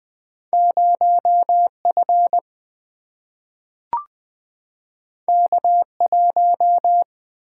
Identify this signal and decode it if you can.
Morse code, heard as a keyed tone, about 20 wpm, 700 Hz. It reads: 0F K1
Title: Single-Letter Single-Number 20wpm